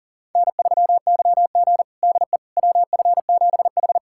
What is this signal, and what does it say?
Morse code, heard as a keyed tone, about 37 wpm, 700 Hz. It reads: N3YG DE WF7H
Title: USA Callsign DE Callsign - 4 Characters - 40wpm